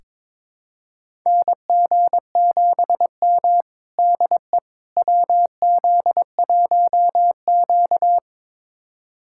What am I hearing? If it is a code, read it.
Morse code, heard as a keyed tone, about 22 wpm, 700 Hz. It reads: NG7M DE WZ1Q
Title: USA Callsign DE Callsign - 4 Characters - 22wpm